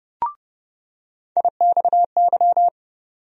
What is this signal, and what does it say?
Morse code, heard as a keyed tone, about 30 wpm, 700 Hz. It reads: IXY
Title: USA Call Sign Postfixes - 3 Letters 30wpm